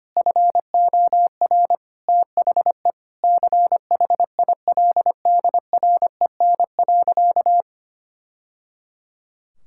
Morse code, heard as a keyed tone, about 25 wpm, 700 Hz. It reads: FOR THE CHILDREN.
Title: Sentences from Top 1000 Words - Review Part 6 - 25wpm